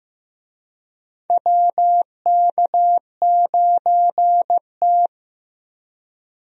Morse code, heard as a keyed tone, about 15 wpm, 700 Hz. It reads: WK9T
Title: USA Call Signs - 4 Characters - Repeated 2x - 15wpm